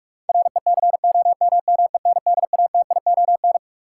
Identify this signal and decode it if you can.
Morse code, heard as a keyed tone, about 46 wpm, 700 Hz. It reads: RECOMMENDATION
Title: Sentences from Top 1400 Words - New Word Drill - 45wpm